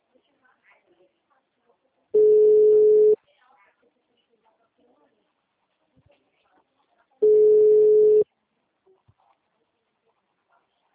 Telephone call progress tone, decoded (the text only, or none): ringback tone